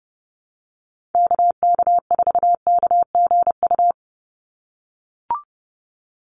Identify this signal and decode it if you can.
Morse code, heard as a keyed tone, about 30 wpm, 700 Hz. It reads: KK4KGU
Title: USA Call Signs 30wpm